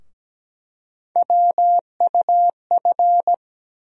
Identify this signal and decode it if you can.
Morse code, heard as a keyed tone, about 17 wpm, 700 Hz. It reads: WUF
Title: USA Call Sign Postfixes - 3 Letters 17wpm